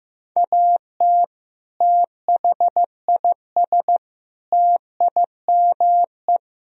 Morse code, heard as a keyed tone, about 16 wpm, 700 Hz. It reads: AT THIS TIME
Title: Sentences from Top 300 Words and Abbreviations 15wpm